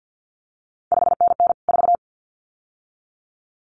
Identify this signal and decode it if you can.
Morse code, heard as a keyed tone, about 50 wpm, 700 Hz. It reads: 5NN 4
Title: CQ WW DX Contest Exchange - 50wpm